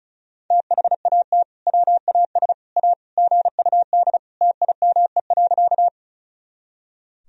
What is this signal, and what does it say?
Morse code, heard as a keyed tone, about 36 wpm, 700 Hz. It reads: THAT WAS A GUD TIME.
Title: Sentences from Top 200 Words and Abbreviations 35wpm